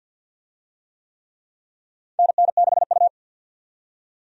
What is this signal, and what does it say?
Morse code, heard as a keyed tone, about 50 wpm, 700 Hz. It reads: NN6U
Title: USA Call Signs - 4 Characters - 50wpm